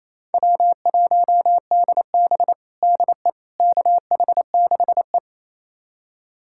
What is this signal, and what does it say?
Morse code, heard as a keyed tone, about 28 wpm, 700 Hz. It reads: W1DB DE KH6E
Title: USA Callsign DE Callsign - 4 Characters - 28wpm